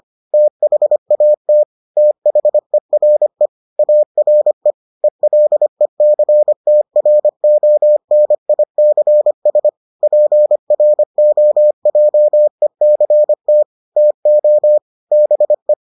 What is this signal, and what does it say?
Morse code, heard as a keyed tone, about 25 wpm, 600 Hz. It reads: THAT THERE ARE ELECTRONICS PROJECT TO BE